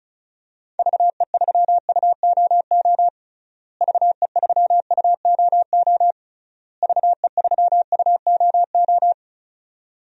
Morse code, heard as a keyed tone, about 36 wpm, 700 Hz. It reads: VE3UOO VE3UOO VE3UOO
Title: DXCC-100 Call Signs - 6 Characters - Repeated 3x - 35wpm